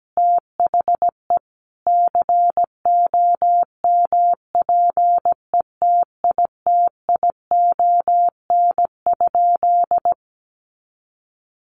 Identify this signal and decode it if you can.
Morse code, heard as a keyed tone, about 17 wpm, 700 Hz. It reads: THE COMPETITION?
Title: Sentences from Top 1000 Words 17wpm